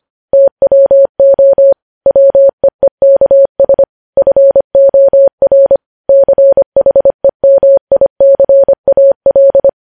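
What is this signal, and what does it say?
Morse code, heard as a keyed tone, about 25 wpm, 550 Hz. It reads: TWO WEEKS FOR CHEMICAL
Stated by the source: Morse news headlines sent at 25 words per minute.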